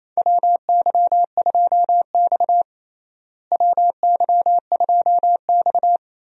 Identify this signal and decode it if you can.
Morse code, heard as a keyed tone, about 28 wpm, 700 Hz. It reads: WY2X WY2X
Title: USA Call Signs - 4 Characters - Repeated 2x - 28wpm